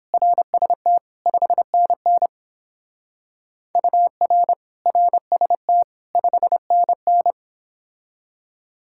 Morse code, heard as a keyed tone, about 28 wpm, 700 Hz. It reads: RST 5NN UR RST 5NN